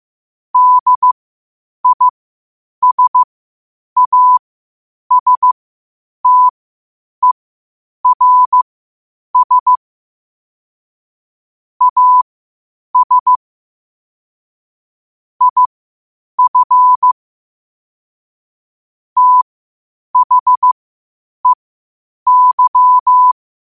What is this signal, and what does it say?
Morse code, heard as a keyed tone, about 15 wpm, spaced as for 8 wpm, 1000 Hz. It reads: DISASTERS AS IF THEY
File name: qotdfiles / 8wpm